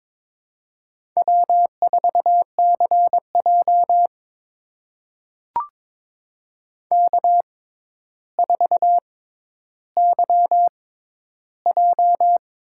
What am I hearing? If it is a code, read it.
Morse code, heard as a keyed tone, about 22 wpm, spaced as for 11 wpm, 700 Hz. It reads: W4CJ K4YJ
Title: USA Call Signs - 4 Characters - 6x Character Spacing - 22wpm